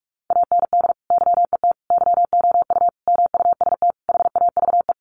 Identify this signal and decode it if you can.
Morse code, heard as a keyed tone, about 44 wpm, 700 Hz. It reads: AND YET YOU MUST HAVE
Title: Sentences from Top 1300 Words 45wpm